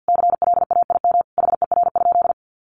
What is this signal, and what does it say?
Morse code, heard as a keyed tone, about 50 wpm, 700 Hz. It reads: CLAIM HER?
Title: Sentences from Top 1500 Words 50wpm